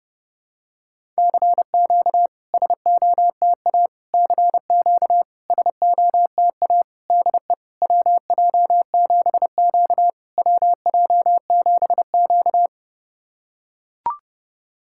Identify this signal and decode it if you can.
Morse code, heard as a keyed tone, about 30 wpm, 700 Hz. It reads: CQ SOTA CQ SOTA DE WJ7Q WJ7Q
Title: SOTA QSO Practice 30wpm